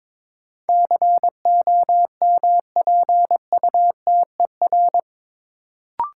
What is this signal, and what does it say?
Morse code, heard as a keyed tone, about 22 wpm, 700 Hz. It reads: COMPUTER
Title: Sentences from Top 1000 Words - Review Part 3 - New Word Drill - 22wpm